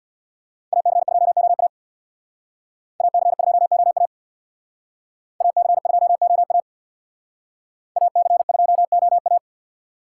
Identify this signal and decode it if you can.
Morse code, heard as a keyed tone, about 63 wpm, 700 Hz. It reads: AC2OA AC2OA AC2OA AC2OA